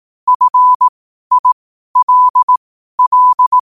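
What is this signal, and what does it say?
Morse code, heard as a keyed tone, about 18 wpm, spaced as for 12 wpm, 1000 Hz. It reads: FILL
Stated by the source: Morse code at 12 wpm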